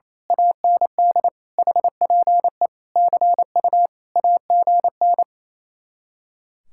Morse code, heard as a keyed tone, about 26 wpm, 700 Hz. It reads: AND HPE CU AGN
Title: Speed-Racing Common QSO Phrases - 28wpm